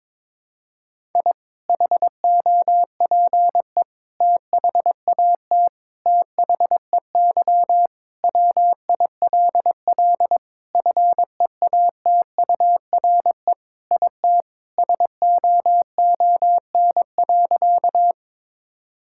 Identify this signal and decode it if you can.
Morse code, heard as a keyed tone, about 22 wpm, 700 Hz. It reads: I HOPE THAT THEY WILL FEATURE IT SOON.
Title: Sentences from Top 1000 Words - Review Part 5 - 22wpm